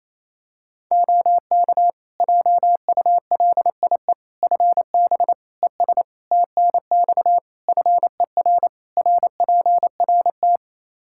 Morse code, heard as a keyed tone, about 28 wpm, 700 Hz. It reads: OK JULIE FB ES TNX FER RPRT
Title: Practice QSO - Exchange 2 - Info - 28wpm